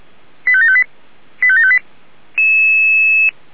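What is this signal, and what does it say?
Five-tone selective call (ZVEI), heard as quick runs of five tones, tones of about 70 ms. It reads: 85758 85758
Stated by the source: ZVEI five-tone selective call received by SDR.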